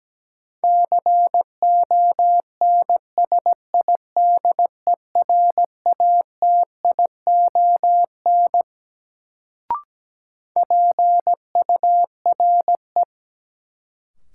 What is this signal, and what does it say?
Morse code, heard as a keyed tone, about 17 wpm, 700 Hz. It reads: CONSIDERATION PURE
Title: Sentences from Top 1500 Words - New Word Drill Alt - 17wpm